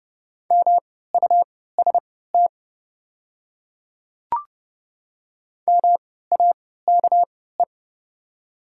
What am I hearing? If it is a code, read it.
Morse code, heard as a keyed tone, about 30 wpm, spaced as for 17 wpm, 690 Hz. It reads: MUST MAKE
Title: 3x Character Spacing - Top 200 Words - 30wpm